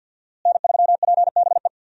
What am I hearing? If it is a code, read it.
Morse code, heard as a keyed tone, about 51 wpm, 700 Hz. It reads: N3PBE
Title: USA Call Signs - 5 Characters - Repeated 3x - 50wpm